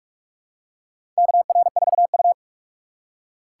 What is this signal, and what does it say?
Morse code, heard as a keyed tone, about 44 wpm, 700 Hz. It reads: KR4U
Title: USA Call Signs - 4 Characters - 6x Character Spacing - 45wpm